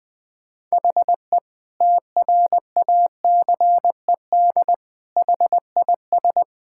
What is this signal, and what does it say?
Morse code, heard as a keyed tone, about 20 wpm, 700 Hz. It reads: HE TRACED HIS
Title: Sentences with 6-Word Phrases - 20wpm